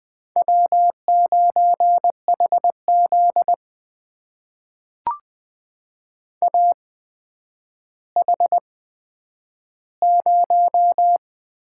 Morse code, heard as a keyed tone, about 20 wpm, spaced as for 10 wpm, 700 Hz. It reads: W9HZ AH0
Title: USA Call Signs - 4 Characters - 8x Character Spacing - 20wpm